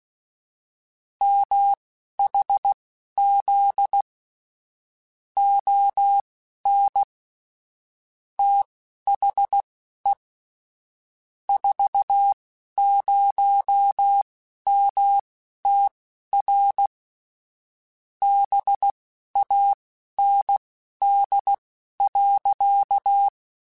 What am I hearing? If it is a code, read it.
Morse code, heard as a keyed tone, about 16 wpm, spaced as for 11 wpm, 800 Hz. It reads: MHZ ON THE 40MTR BAND.